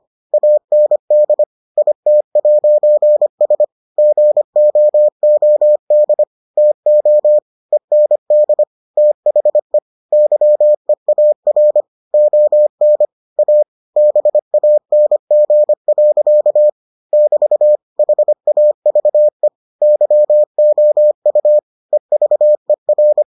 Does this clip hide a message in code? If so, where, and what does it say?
Yes: Morse code from 0.3 to 23.2 s, heard as a keyed tone, about 25 wpm, 600 Hz. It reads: AND IT'S GOOD TO END THE YEAR ON A BANG. <BT> HAVE YOU EVER